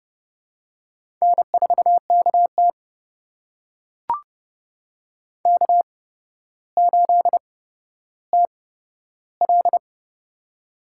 Morse code, heard as a keyed tone, about 30 wpm, spaced as for 13 wpm, 700 Hz. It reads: N4KT K8TL
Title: USA Call Signs - 4 Characters - 8x Character Spacing - 30wpm